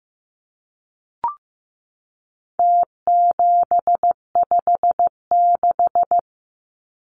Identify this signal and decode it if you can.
Morse code, heard as a keyed tone, about 15 wpm, 700 Hz. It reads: T756
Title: Numbers - Contest Style - 4-Digit 15wpm